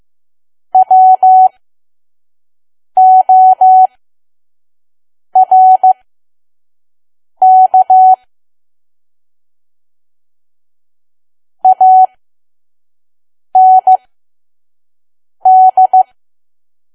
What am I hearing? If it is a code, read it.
Morse code, heard as a keyed tone, about 15 wpm, spaced as for 5 wpm, 750 Hz. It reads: WORK AND
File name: Archive / 5 WPM files / PM